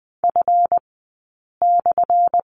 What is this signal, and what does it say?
Morse code, heard as a keyed tone, about 20 wpm, spaced as for 8 wpm, 700 Hz. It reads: F/
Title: 211 Lesson 34d - Number 8 - Two Characters 20wpm